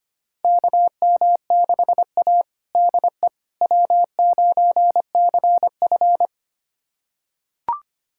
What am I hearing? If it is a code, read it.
Morse code, heard as a keyed tone, about 25 wpm, 700 Hz. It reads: KM6A DE W9CF
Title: USA Callsign DE Callsign - 4 Characters - 25wpm